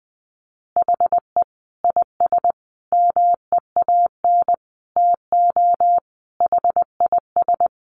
Morse code, heard as a keyed tone, about 20 wpm, 700 Hz. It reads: HE IS MEAN TO HIS
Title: Additional Easy Phrases 20wpm